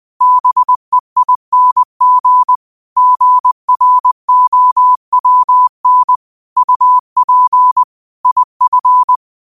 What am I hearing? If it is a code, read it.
Morse code, heard as a keyed tone, about 20 wpm, 1000 Hz. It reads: BEING GROWN UP IF
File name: qotdfiles / 20wpm